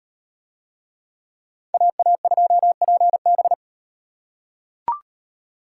Morse code, heard as a keyed tone, about 36 wpm, 700 Hz. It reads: AA2PB